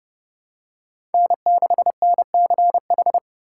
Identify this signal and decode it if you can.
Morse code, heard as a keyed tone, about 30 wpm, 700 Hz. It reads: N6NCH
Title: USA Call Signs - 5 Characters - 6x Character Spacing - 30wpm